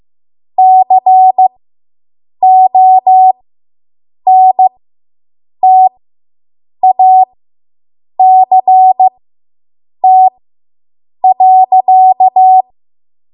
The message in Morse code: CONTACT.